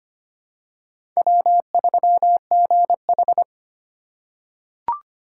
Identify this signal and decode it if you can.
Morse code, heard as a keyed tone, about 25 wpm, 700 Hz. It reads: W3GH
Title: USA Call Signs - 4 Characters - 4x Character Spacing - 25wpm